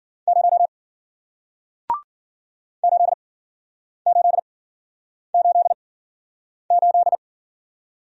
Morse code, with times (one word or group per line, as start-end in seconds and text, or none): 0.3-0.7 0
2.8-7.2 8888